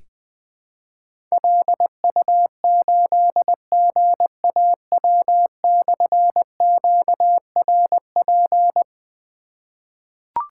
Morse code, heard as a keyed tone, about 21 wpm, 700 Hz. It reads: LU8GAW/QRP
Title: Difficult Call Signs 20wpm